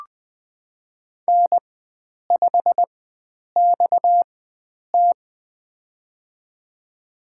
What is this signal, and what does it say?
Morse code, heard as a keyed tone, about 21 wpm, spaced as for 9 wpm, 700 Hz. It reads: N5XT
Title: USA Call Signs - 4 Characters - 4x Character Spacing - 20wpm